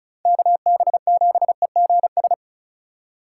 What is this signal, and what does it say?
Morse code, heard as a keyed tone, about 36 wpm, 700 Hz. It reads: KB7EGS
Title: USA Call Signs - 8x Character Spacing - 35wpm